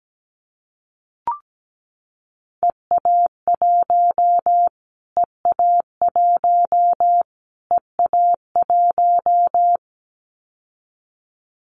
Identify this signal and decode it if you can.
Morse code, heard as a keyed tone, about 17 wpm, 700 Hz. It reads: EA1 EA1 EA1
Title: Callsign Prefixes for Top 25 DXCC - Repeated 3x - 17wpm